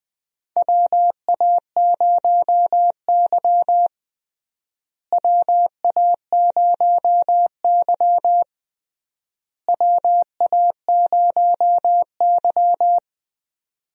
Morse code, heard as a keyed tone, about 20 wpm, 700 Hz. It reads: WA0Y WA0Y WA0Y
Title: USA Call Signs - 4 Characters - Repeated 3x - 20wpm